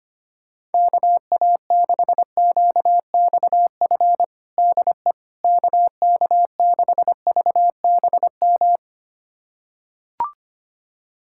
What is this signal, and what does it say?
Morse code, heard as a keyed tone, about 25 wpm, 700 Hz. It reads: KA6QXF DE KK6VBM
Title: USA Callsign DE Callsign - 6 Characters - 25wpm